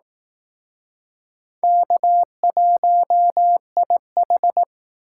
Morse code, heard as a keyed tone, about 19 wpm, 700 Hz. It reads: K1IH